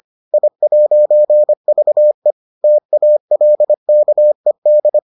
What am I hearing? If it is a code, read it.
Morse code, heard as a keyed tone, about 25 wpm, 600 Hz. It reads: I'VE TALKED